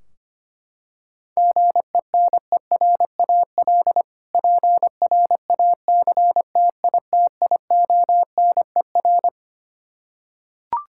Morse code, heard as a keyed tone, about 25 wpm, 700 Hz. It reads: GENERAL PRACTITIONER
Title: K1USN SST QSO Practice 25wpm